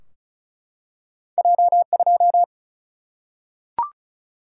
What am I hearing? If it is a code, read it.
Morse code, heard as a keyed tone, about 36 wpm, 700 Hz. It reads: J2
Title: Single-Letter Single-Number 35wpm